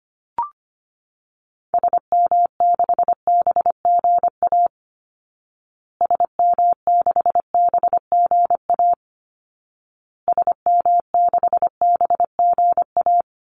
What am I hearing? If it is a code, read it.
Morse code, heard as a keyed tone, about 25 wpm, 700 Hz. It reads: SM6BGA SM6BGA SM6BGA
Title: Call Signs Repeated 2x - 25wpm